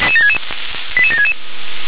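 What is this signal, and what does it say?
Five-tone selective call (ZVEI), heard as quick runs of five tones, tones of about 70 ms. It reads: 88766
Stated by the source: ZVEI five-tone call 88690 received by SDR.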